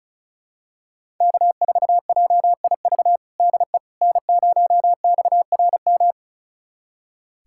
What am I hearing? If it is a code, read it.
Morse code, heard as a keyed tone, about 36 wpm, 700 Hz. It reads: K4JIV DE N0XRM
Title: USA Callsign DE Callsign - 5 Characters - 35wpm